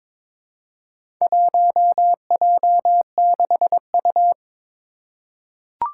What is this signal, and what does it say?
Morse code, heard as a keyed tone, about 22 wpm, 700 Hz. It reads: 1J6U
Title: Four Challenging Letter-Number Combinations 22wpm